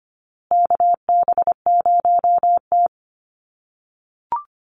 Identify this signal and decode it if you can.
Morse code, heard as a keyed tone, about 25 wpm, 700 Hz. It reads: KB0T
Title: USA Call Signs Repeated 3x 25wpm